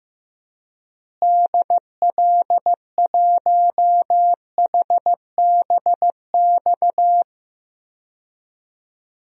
Morse code, heard as a keyed tone, about 15 wpm, 700 Hz. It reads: DL1HBX
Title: DXCC-100 Call Signs - 6 Characters - Repeated 2x - 15wpm